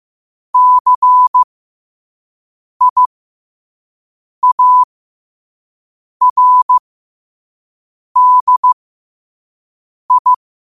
Morse code, heard as a keyed tone, about 15 wpm, spaced as for 5 wpm, 1000 Hz. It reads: CIARDI
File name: qotdfiles / 5wpm